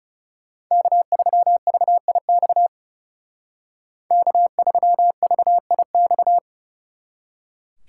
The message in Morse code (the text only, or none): K3VIX K3VIX